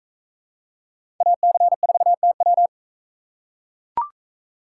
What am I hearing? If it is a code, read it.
Morse code, heard as a keyed tone, about 39 wpm, 700 Hz. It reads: AC4TW